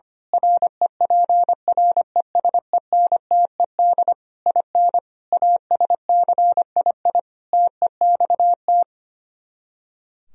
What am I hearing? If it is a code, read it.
Morse code, heard as a keyed tone, about 24 wpm, 700 Hz. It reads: REPRESENTED IN ASCII TEXT
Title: Speed-Racing - Sets of 5 Words 25wpm